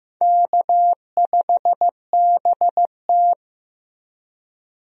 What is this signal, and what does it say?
Morse code, heard as a keyed tone, about 15 wpm, 700 Hz. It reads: K5BT
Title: USA Call Signs - 4 Characters - Repeated 3x - 15wpm